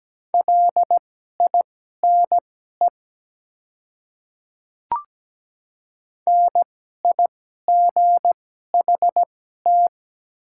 Morse code, heard as a keyed tone, about 17 wpm, spaced as for 12 wpm, 690 Hz. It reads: LINE NIGHT
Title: 2x Character Spacing - Top 300 Words - 17wpm